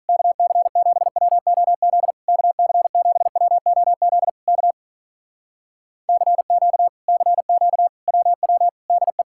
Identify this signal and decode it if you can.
Morse code, heard as a keyed tone, about 46 wpm, 700 Hz. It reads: KC7WOZ KC7WOZ K CQ CQ WW DE